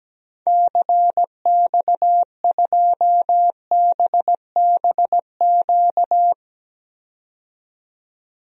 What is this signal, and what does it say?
Morse code, heard as a keyed tone, about 17 wpm, 700 Hz. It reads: CX2BBQ
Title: DXCC-100 Call Signs - 6 Characters - 17wpm